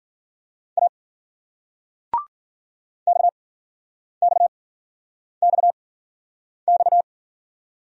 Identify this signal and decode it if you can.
Morse code, heard as a keyed tone, about 51 wpm, spaced as for 9 wpm, 700 Hz. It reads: A XXXX